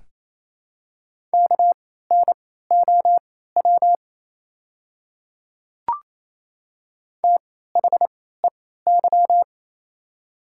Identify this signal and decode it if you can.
Morse code, heard as a keyed tone, about 28 wpm, spaced as for 16 wpm, 700 Hz. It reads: KNOW THEY
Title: 3x Character Spacing - Top 100 Words - 28wpm